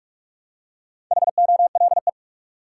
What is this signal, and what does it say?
Morse code, heard as a keyed tone, about 46 wpm, 700 Hz. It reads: HOLE
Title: Four Letter Words - 45wpm